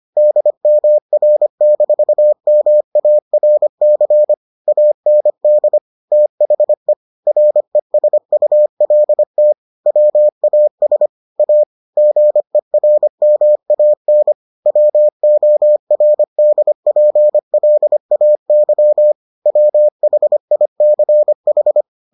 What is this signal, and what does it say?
Morse code, heard as a keyed tone, about 25 wpm, 600 Hz. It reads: DMR-MARC AND THE RESULT WAS A GERMAN WORDPLAY WHICH